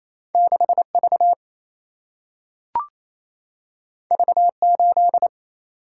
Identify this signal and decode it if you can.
Morse code, heard as a keyed tone, about 28 wpm, 700 Hz. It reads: 6V V8
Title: Single-Letter Single-Number 28wpm